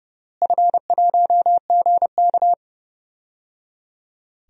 Morse code, heard as a keyed tone, about 29 wpm, 700 Hz. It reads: F1GK
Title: DXCC-100 Call Signs - 4 Characters - Repeated 3x - 30wpm